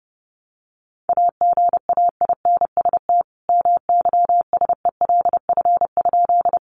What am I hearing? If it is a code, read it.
Morse code, heard as a keyed tone, about 30 wpm, 700 Hz. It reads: AGAINST MYSELF?
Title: Sentences from Top 1300 Words 30wpm